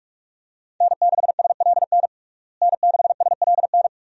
Morse code, heard as a keyed tone, about 46 wpm, 700 Hz. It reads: N6SLN N6SLN
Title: USA Call Signs - 5 Characters - Repeated 2x - 45wpm